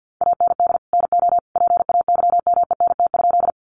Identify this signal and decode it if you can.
Morse code, heard as a keyed tone, about 50 wpm, 700 Hz. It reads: AND NO PAYMENT?
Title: Sentences from Top 1200 Words 50wpm